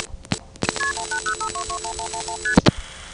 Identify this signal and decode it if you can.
Multi-frequency trunk signalling, heard as fast pairs of tones, tones of about 90 ms. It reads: KP1805221111ST